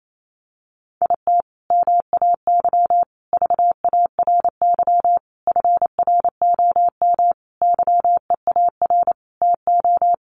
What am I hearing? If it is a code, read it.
Morse code, heard as a keyed tone, about 28 wpm, 700 Hz. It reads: IT MAY VARY FROM YEAR TO
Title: Sentences from Top 1200 Words 28wpm